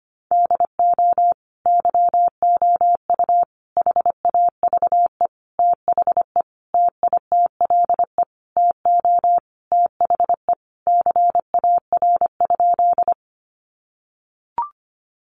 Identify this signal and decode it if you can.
Morse code, heard as a keyed tone, about 25 wpm, 700 Hz. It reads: DO YOU HAVE THE TITLE TO THE CAR?
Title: Sentences from Top 1200 Words 25wpm